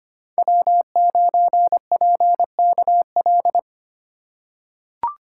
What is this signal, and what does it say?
Morse code, heard as a keyed tone, about 25 wpm, 700 Hz. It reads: W9PKL
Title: USA Call Signs - 5 Characters - 6x Character Spacing - 25wpm